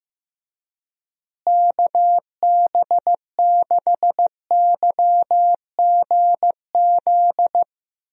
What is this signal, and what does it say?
Morse code, heard as a keyed tone, about 15 wpm, 700 Hz. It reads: KB6YGZ
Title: USA Call Signs - 6 Characters - Repeated 2x - 15wpm